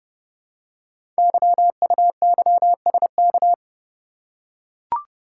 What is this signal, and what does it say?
Morse code, heard as a keyed tone, about 30 wpm, 690 Hz. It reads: YUYSK
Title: Code Groups - 5 Letters - 30wpm